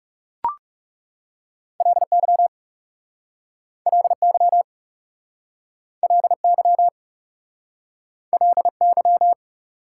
Morse code, heard as a keyed tone, about 41 wpm, 700 Hz. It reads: LY LY LY LY